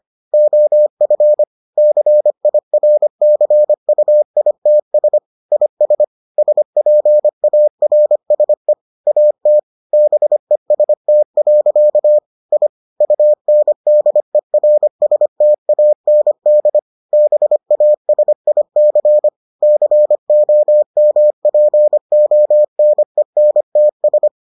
Morse code, heard as a keyed tone, about 25 wpm, 600 Hz. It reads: OF CIRCUITS IS SPARSE AT BEST. I UNDERSTAND BASIC COMPONENTS